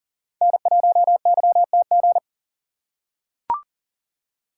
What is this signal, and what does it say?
Morse code, heard as a keyed tone, about 40 wpm, 700 Hz. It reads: N1YTG
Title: USA Call Signs - 5 Characters - 2x Character Spacing - 40wpm